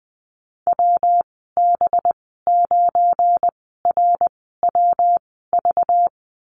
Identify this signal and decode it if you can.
Morse code, heard as a keyed tone, about 20 wpm, spaced as for 15 wpm, 700 Hz. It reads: WB9RWV
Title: 193 Lesson 31g - Letter v - Call Signs 20wpm